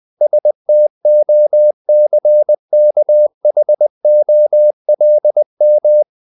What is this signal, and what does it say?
Morse code, heard as a keyed tone, about 20 wpm, 600 Hz. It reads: STOCKHOLM
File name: CW / Practice / NationalCapitals20